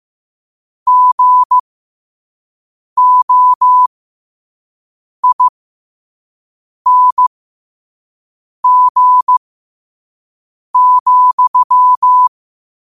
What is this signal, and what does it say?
Morse code, heard as a keyed tone, about 15 wpm, spaced as for 5 wpm, 1000 Hz. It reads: GOING,
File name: qotdfiles / 5wpm